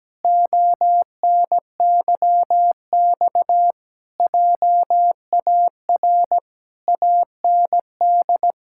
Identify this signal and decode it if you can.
Morse code, heard as a keyed tone, about 17 wpm, 700 Hz. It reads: ONYX JAR AND
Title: Pangram Challenge v2 - 17wpm